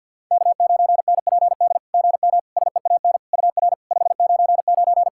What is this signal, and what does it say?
Morse code, heard as a keyed tone, about 49 wpm, 690 Hz. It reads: K9NPD GM SEAN UR 599
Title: SOTA QSO Practice 50wpm